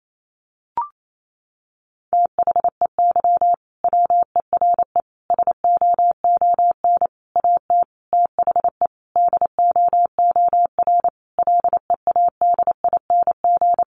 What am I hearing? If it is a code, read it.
Morse code, heard as a keyed tone, about 28 wpm, 700 Hz. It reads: THEY WERE SOON AT THE DOOR LEADING
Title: Sentences from Top 1200 Words 28wpm